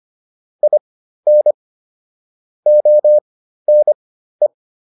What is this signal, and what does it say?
Morse code, heard as a keyed tone, about 25 wpm, spaced as for 13 wpm, 600 Hz. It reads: IN ONE
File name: WD302 13